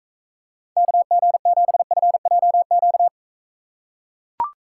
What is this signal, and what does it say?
Morse code, heard as a keyed tone, about 42 wpm, 700 Hz. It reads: KG7FJQ